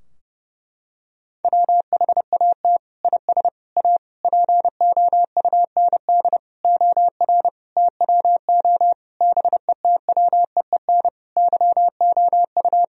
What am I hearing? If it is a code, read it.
Morse code, heard as a keyed tone, about 30 wpm, 700 Hz. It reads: WHAT IS A POUND OR TWO BETWEEN YOU
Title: Sentences from Top 1000 Words - Review Part 2 - 30wpm